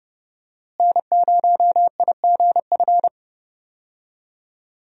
Morse code, heard as a keyed tone, about 30 wpm, 700 Hz. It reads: N0IGF
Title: USA Call Signs - 5 Characters - Repeated 3x - 30wpm